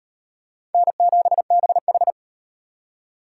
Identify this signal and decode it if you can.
Morse code, heard as a keyed tone, about 36 wpm, 700 Hz. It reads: N7BH